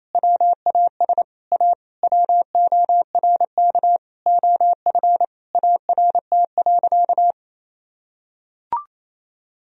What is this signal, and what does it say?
Morse code, heard as a keyed tone, about 28 wpm, 700 Hz. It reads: WAS A WORK OF ART.
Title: Sentences from Top 1000 Words - Review Part 3 - 28wpm